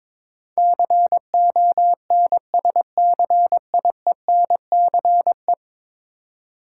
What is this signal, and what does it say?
Morse code, heard as a keyed tone, about 22 wpm, 700 Hz. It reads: CONSCIENCE
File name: practice / 22wpm / 10 Letter Words - First 3 Letters Given - 22wpm